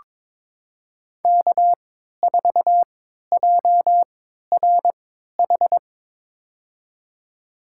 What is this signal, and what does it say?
Morse code, heard as a keyed tone, about 21 wpm, spaced as for 12 wpm, 700 Hz. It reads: K4JRH
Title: USA Call Signs - 5 Characters - 3x Character Spacing - 22wpm